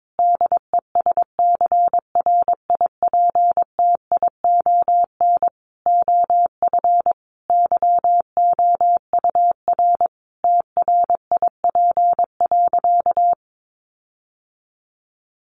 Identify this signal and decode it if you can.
Morse code, heard as a keyed tone, about 22 wpm, 700 Hz. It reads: DESCRIPTION OF YOUR TRIP.
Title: Sentences from Top 1500 Words 22wpm